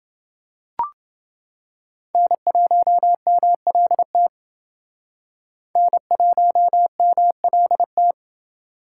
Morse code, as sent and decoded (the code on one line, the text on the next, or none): -. .---- -- .-.. - / -. .---- -- .-.. -
N1MLT N1MLT